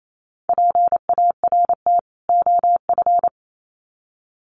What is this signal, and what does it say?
Morse code, heard as a keyed tone, about 28 wpm, 700 Hz. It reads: PART OF
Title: Sets of 2 Words 28wpm